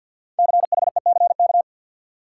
Morse code, heard as a keyed tone, about 53 wpm, 700 Hz. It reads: CHECK
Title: Sentences from Top 1000 Words - Review Part 3 - New Word Drill - 50wpm